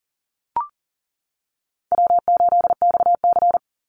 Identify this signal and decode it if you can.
Morse code, heard as a keyed tone, about 40 wpm, 700 Hz. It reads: W8XC
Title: Easy Call Signs - 40wpm